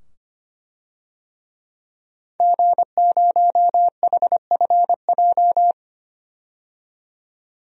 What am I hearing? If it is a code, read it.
Morse code, heard as a keyed tone, about 25 wpm, 700 Hz. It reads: G0HFJ
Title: DXCC-100 Call Signs - 5 Characters - 25wpm